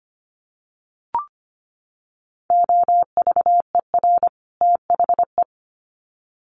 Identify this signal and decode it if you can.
Morse code, heard as a keyed tone, about 25 wpm, 700 Hz. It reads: OVER THE
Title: Sets of 2 Words 25wpm